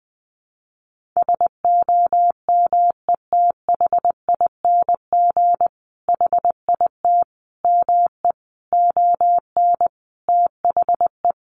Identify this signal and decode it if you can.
Morse code, heard as a keyed tone, about 20 wpm, 700 Hz. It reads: SOMETHING HIT ME ON THE
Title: Sentences from Top 1200 Words 20wpm